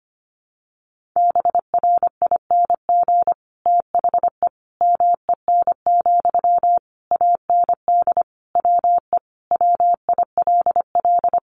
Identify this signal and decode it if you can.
Morse code, heard as a keyed tone, about 25 wpm, 700 Hz. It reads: BRING THE MEN, AND WE WILL
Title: Sentences from Top 1400 Words 25wpm